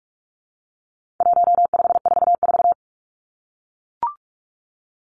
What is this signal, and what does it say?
Morse code, heard as a keyed tone, about 44 wpm, 700 Hz. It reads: 1544
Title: Numbers - 4-Digit 45wpm